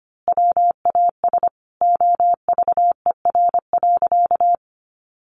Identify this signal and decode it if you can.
Morse code, heard as a keyed tone, about 25 wpm, 700 Hz. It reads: WAS OVER.
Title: Sentences from Top 200 Words 25wpm